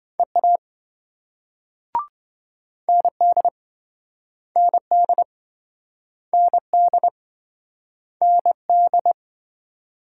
Morse code, heard as a keyed tone, about 27 wpm, 700 Hz. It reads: EA ND ND ND ND